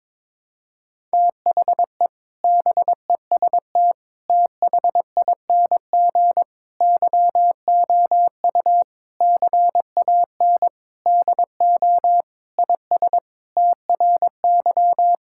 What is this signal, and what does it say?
Morse code, heard as a keyed tone, about 22 wpm, 700 Hz. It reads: THE BEST THING YOU CAN DO IS TRY
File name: Sentences with 7-Word Phrases - 22wpm